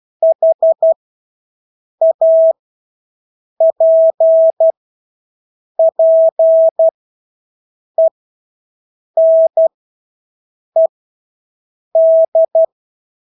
Morse code, heard as a keyed tone, about 12 wpm, spaced as for 6 wpm, 650 Hz. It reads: HAPPENED
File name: Engelske ord 12-6 wpm